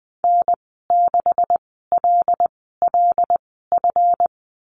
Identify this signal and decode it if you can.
Morse code, heard as a keyed tone, about 20 wpm, spaced as for 14 wpm, 700 Hz. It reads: N6LLF
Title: 161 Lesson 26g - Letter g - Call Signs 20wpm